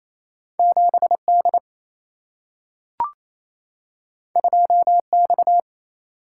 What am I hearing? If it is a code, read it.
Morse code, heard as a keyed tone, about 28 wpm, 700 Hz. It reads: 7D 2X
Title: Single-Letter Single-Number 28wpm